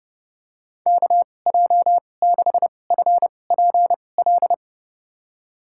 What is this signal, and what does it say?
Morse code, heard as a keyed tone, about 32 wpm, spaced as for 22 wpm, 700 Hz. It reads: KJ6FPL
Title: USA Call Signs - 6 Characters - 2x Character Spacing - 30wpm